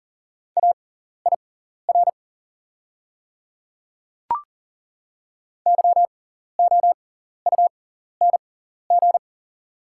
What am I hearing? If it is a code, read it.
Morse code, heard as a keyed tone, about 40 wpm, spaced as for 14 wpm, 700 Hz. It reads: AIR YOUNG
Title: 6x Character Spacing - Top 500 Words - 40wpm